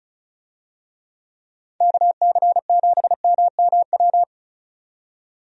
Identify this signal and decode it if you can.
Morse code, heard as a keyed tone, about 34 wpm, 700 Hz. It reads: KC7MMW
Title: USA Call Signs - 6 Characters - 35wpm